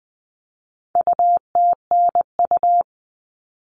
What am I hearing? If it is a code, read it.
Morse code, heard as a keyed tone, about 20 wpm, 700 Hz. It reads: UTNU
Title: All Cut Numbers - 4-Digit 20wpm